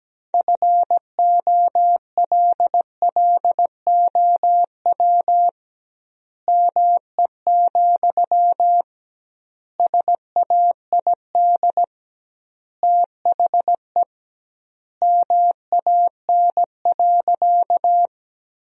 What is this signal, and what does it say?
Morse code, heard as a keyed tone, about 17 wpm, 700 Hz. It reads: FOLLOW ME, SAID THE MAN.
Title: 2x Word-Spacing - Sentences from Top 200 Words - 17wpm